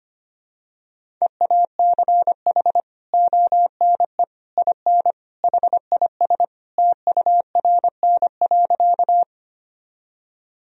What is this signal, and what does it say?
Morse code, heard as a keyed tone, about 25 wpm, 700 Hz. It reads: EACH ONE IN HIS TURN.
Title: Sentences from Top 1000 Words - Review Part 1 - 25wpm